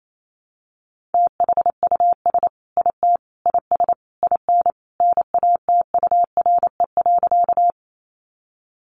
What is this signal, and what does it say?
Morse code, heard as a keyed tone, about 28 wpm, 700 Hz. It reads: THUS IT IS IN NATURE.
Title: Sentences from Top 800 Words 28wpm